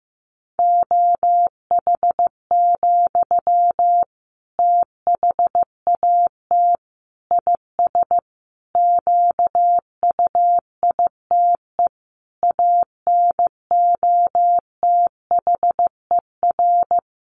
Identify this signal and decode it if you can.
Morse code, heard as a keyed tone, about 15 wpm, 700 Hz. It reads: OH, THAT IS QUITE ANOTHER
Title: Sentences from Top 700 Words 15wpm